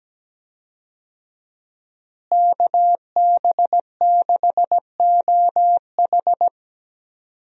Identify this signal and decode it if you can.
Morse code, heard as a keyed tone, about 17 wpm, 700 Hz. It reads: KB6OH
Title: USA Call Signs - 5 Characters - Repeated 2x - 17wpm